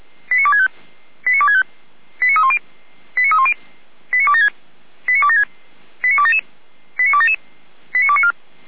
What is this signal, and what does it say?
Five-tone selective call (ZVEI), heard as quick runs of five tones, tones of about 80 ms. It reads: 89265 89265 89310 89310 89276 89276 89277 89277 89294